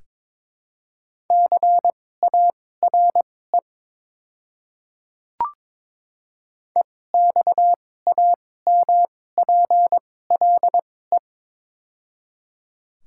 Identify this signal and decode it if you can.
Morse code, heard as a keyed tone, about 21 wpm, spaced as for 16 wpm, 700 Hz. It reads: CARE EXAMPLE
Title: 2x Character Spacing - Top 300 Words - 22wpm